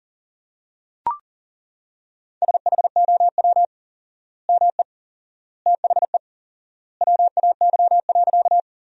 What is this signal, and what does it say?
Morse code, heard as a keyed tone, about 37 wpm, 700 Hz. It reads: SHOW ME THE WAY.
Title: 4x Word-Spacing - Sentences from Top 200 Words - 40wpm